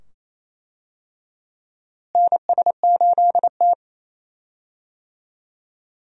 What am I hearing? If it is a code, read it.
Morse code, heard as a keyed tone, about 28 wpm, 700 Hz. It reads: NS8T
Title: USA Call Signs - 4 Characters - Repeated 2x - 28wpm